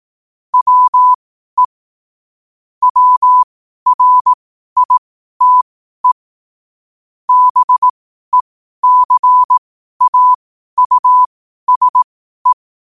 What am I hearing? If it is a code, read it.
Morse code, heard as a keyed tone, about 18 wpm, spaced as for 12 wpm, 1000 Hz. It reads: WE WRITE BECAUSE
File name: qotdfiles / 12wpm